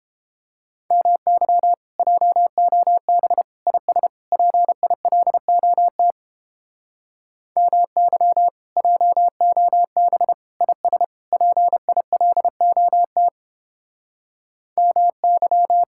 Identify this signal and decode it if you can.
Morse code, heard as a keyed tone, about 29 wpm, 700 Hz. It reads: MY JOB IS PILOT MY JOB IS PILOT MY